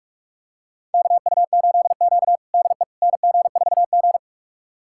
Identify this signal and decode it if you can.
Morse code, heard as a keyed tone, about 46 wpm, 700 Hz. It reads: KU8Q DE NG4G
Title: USA Callsign DE Callsign - 4 Characters - 45wpm